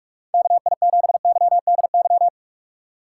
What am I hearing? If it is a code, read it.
Morse code, heard as a keyed tone, about 46 wpm, 700 Hz. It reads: KI7YDY
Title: USA Call Signs - 3x Character Spacing - 45wpm